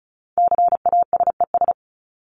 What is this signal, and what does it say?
Morse code, heard as a keyed tone, about 34 wpm, 700 Hz. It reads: CASES
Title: ES:S Suffix - Top 100 Words with Base Words 35wpm